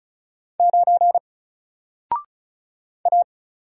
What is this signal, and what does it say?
Morse code, heard as a keyed tone, about 36 wpm, 700 Hz. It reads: 9 A
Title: Single Letter-Number Alt 35wpm